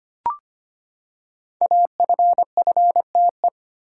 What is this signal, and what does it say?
Morse code, heard as a keyed tone, about 25 wpm, 700 Hz. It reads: AFFTE
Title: Code Groups - 5 Letters - 25wpm